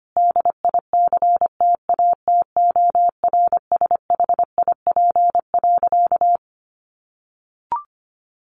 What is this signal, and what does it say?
Morse code, heard as a keyed tone, about 25 wpm, 700 Hz. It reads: DICTATORSHIP.
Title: A Sound Of Thunder 25wpm